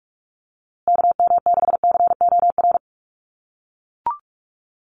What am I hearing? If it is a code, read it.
Morse code, heard as a keyed tone, about 44 wpm, 700 Hz. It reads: KM6COR
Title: USA Call Signs Repeated 3x 45wpm